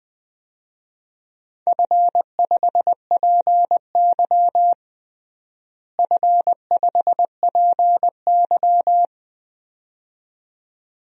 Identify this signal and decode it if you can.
Morse code, heard as a keyed tone, about 21 wpm, 700 Hz. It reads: F5PY F5PY
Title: DXCC-100 Call Signs - 4 Characters - Repeated 2x - 20wpm